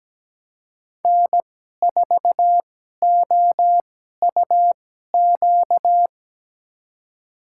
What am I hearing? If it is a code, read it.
Morse code, heard as a keyed tone, about 17 wpm, spaced as for 12 wpm, 700 Hz. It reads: N4OUQ
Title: USA Call Signs - 5 Characters - 2x Character Spacing - 17wpm